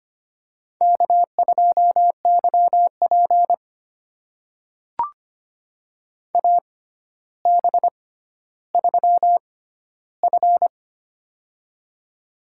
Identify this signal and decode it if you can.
Morse code, heard as a keyed tone, about 25 wpm, spaced as for 13 wpm, 700 Hz. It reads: K2YP AB3F
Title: USA Call Signs - 4 Characters - 6x Character Spacing - 25wpm